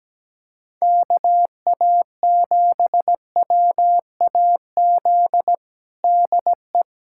Morse code, heard as a keyed tone, about 17 wpm, 700 Hz. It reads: KA7WAZ DE
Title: USA Callsign DE Callsign - 6 Characters - 17wpm